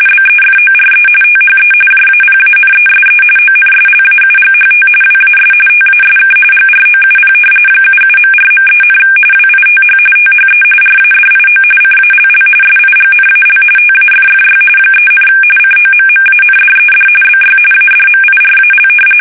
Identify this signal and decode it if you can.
RTTY, about 45.45 baud. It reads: BEFORE THE JAPANESE ONSLAUGHT WAS TASK GROUP 77.4.3, WITH THE RADIO CALL SIGNAL TAFFY 3, COMMANDED BY ADMIRAL
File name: fsk850 / battleshipnj